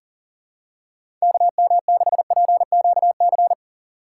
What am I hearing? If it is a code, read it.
Morse code, heard as a keyed tone, about 40 wpm, 700 Hz. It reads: KM6PQC
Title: USA Call Signs - 6 Characters - 2x Character Spacing - 40wpm